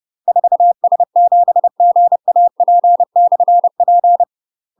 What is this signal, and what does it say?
Morse code, heard as a keyed tone, about 30 wpm, 700 Hz. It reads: 4S7GAP/P
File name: UK MASTER QSOs TRAINING / 30WPM / UK MASTER QSOs 0035